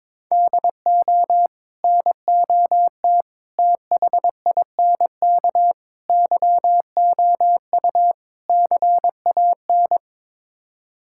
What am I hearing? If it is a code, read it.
Morse code, heard as a keyed tone, about 22 wpm, 700 Hz. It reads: DO NOT THINK YOU CAN
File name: practice / 22wpm / Build A Phrase - 5-Word Phrases - 22wpm